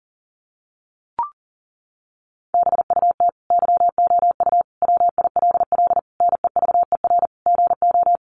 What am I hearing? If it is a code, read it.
Morse code, heard as a keyed tone, about 37 wpm, 700 Hz. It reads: BUT YOU WILL NEVER GO
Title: Sentences from Top 1500 Words 40wpm